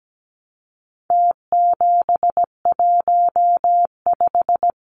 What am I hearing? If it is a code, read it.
Morse code, heard as a keyed tone, about 17 wpm, 700 Hz. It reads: T715
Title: Numbers - Contest Style - 4-Digit 17wpm